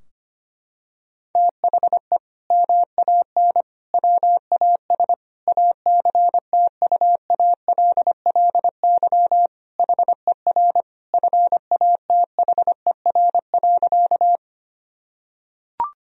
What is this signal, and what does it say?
Morse code, heard as a keyed tone, about 25 wpm, 700 Hz. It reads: THE MAN WAS ACTUALLY HER FATHER.
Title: Sentences from Top 1000 Words - Review Part 4 - 25wpm